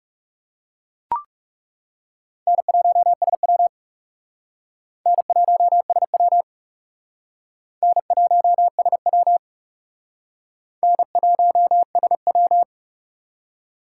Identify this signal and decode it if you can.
Morse code, heard as a keyed tone, about 40 wpm, 700 Hz. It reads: N1SW N1SW N1SW N1SW